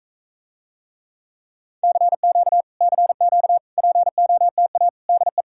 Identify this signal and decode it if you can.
Morse code, heard as a keyed tone, about 44 wpm, 700 Hz. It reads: CQ CQ POTA DE